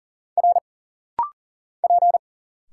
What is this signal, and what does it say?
Morse code, heard as a keyed tone, about 40 wpm, 700 Hz. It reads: R P
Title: Single Letters - Rapid-Fire 40wpm